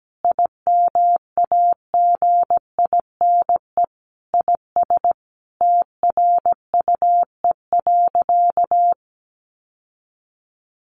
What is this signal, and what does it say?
Morse code, heard as a keyed tone, about 17 wpm, 700 Hz. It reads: IMAGINE IS TRUE.
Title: Sentences from Top 1400 Words 17wpm